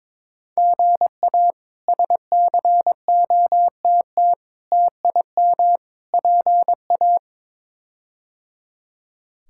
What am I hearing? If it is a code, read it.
Morse code, heard as a keyed tone, about 22 wpm, 700 Hz. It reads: GA SCOTT TIM PA
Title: K1USN SST QSO Practice - 22wpm